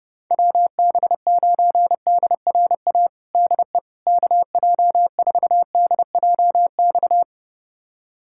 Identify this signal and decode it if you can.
Morse code, heard as a keyed tone, about 32 wpm, 700 Hz. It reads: WB9DRA DE KJ4DJX
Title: USA Callsign DE Callsign - 6 Characters - 30wpm